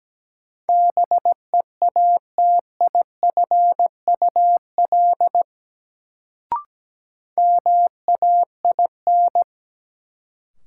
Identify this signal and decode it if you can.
Morse code, heard as a keyed tone, about 17 wpm, 700 Hz. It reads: BEATIFUL MAIN
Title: Sentences from Top 1300 Words - New Word Drill Alt - 17wpm